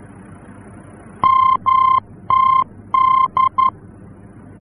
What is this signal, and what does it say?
Morse code, heard as a keyed tone, about 11 wpm, 1050 Hz. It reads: MTD
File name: france / mx / MTD-F-VOR